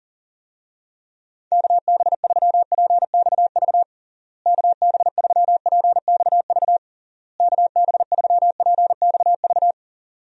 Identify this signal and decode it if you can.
Morse code, heard as a keyed tone, about 40 wpm, 700 Hz. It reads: KB3PXV KB3PXV KB3PXV
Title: USA Call Signs - 6 Characters - Repeated 3x - 40wpm